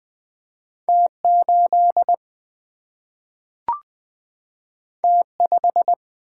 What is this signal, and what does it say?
Morse code, heard as a keyed tone, about 20 wpm, 700 Hz. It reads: T8 T5
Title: Single-Letter Single-Number 20wpm